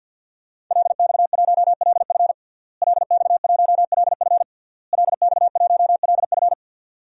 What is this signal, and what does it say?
Morse code, heard as a keyed tone, about 53 wpm, 700 Hz. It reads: LX1LF LX1LF LX1LF
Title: DXCC-100 Call Signs - 5 Characters - Repeated 3x - 50wpm